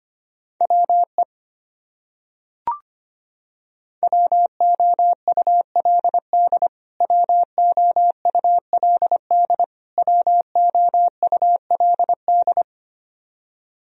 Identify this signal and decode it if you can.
Morse code, heard as a keyed tone, about 25 wpm, 700 Hz. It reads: WE WOULD WOULD WOULD
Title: Top 100 Words Repeated 25wpm